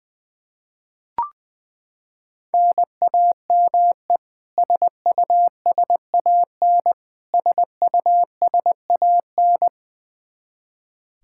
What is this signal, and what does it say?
Morse code, heard as a keyed tone, about 20 wpm, 700 Hz. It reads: NAME SUSAN SUSAN
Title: Practice QSO - Exchange 1 - Beginning - 20wpm